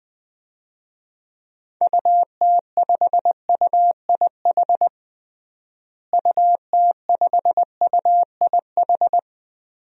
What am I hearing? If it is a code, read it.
Morse code, heard as a keyed tone, about 20 wpm, 700 Hz. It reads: UT5UIH UT5UIH
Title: DXCC-100 Call Signs - 6 Characters - Repeated 2x - 20wpm